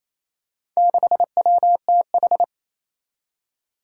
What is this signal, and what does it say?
Morse code, heard as a keyed tone, about 28 wpm, 700 Hz. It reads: 6WTH
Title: Four Challenging Letter-Number Combinations 28wpm